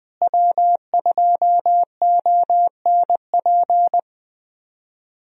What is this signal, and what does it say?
Morse code, heard as a keyed tone, about 20 wpm, 700 Hz. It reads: W2ONP
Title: USA Call Signs - 5 Characters - Repeated 3x - 20wpm